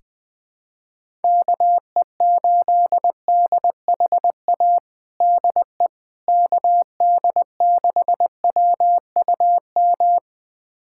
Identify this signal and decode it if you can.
Morse code, heard as a keyed tone, about 21 wpm, 700 Hz. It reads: KE8DHA DE KD6WUM
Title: USA Callsign DE Callsign - 6 Characters - 20wpm